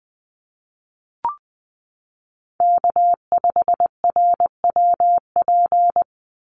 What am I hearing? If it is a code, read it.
Morse code, heard as a keyed tone, about 20 wpm, 700 Hz. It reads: K5RWP
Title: NA SOTA Chaser Callsigns 20wpm